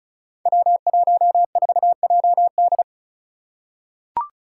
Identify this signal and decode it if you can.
Morse code, heard as a keyed tone, about 36 wpm, 700 Hz. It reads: W14JD
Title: Five Challenging Letter-Number Combinations 35wpm